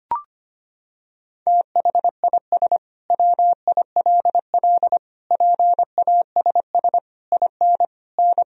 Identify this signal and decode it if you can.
Morse code, heard as a keyed tone, about 25 wpm, 700 Hz. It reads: THIS WILL PASS IN N
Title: Sentences from Top 400 Words and Abbreviations 25wpm